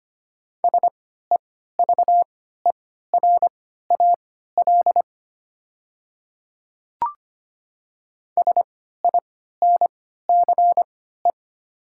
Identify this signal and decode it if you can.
Morse code, heard as a keyed tone, about 25 wpm, spaced as for 14 wpm, 700 Hz. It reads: SEVERAL SINCE
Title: 3x Character Spacing - Top 500 Words - 25wpm